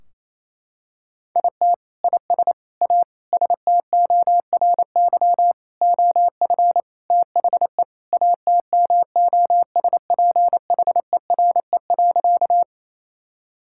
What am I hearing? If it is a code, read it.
Morse code, heard as a keyed tone, about 28 wpm, 700 Hz. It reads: IT IS A STORY OF THE ATMOSPHERE.
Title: Sentences from Top 1600 Words 28wpm